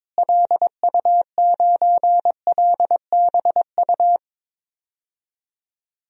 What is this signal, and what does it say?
Morse code, heard as a keyed tone, about 22 wpm, 700 Hz. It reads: LU9LBU
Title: DXCC-100 Call Signs - 6 Characters - Repeated 2x - 22wpm